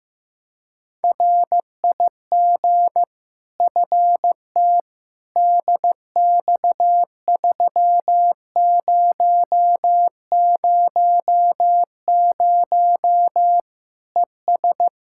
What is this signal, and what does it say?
Morse code, heard as a keyed tone, about 15 wpm, 700 Hz. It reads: RIG FT DX3000 ES
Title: Practice QSO - Exchange 2 - Info - 15wpm